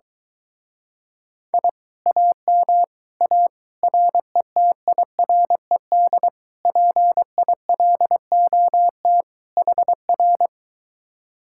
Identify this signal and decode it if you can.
Morse code, heard as a keyed tone, about 23 wpm, 700 Hz. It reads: I AM A RETIRED PILOT HR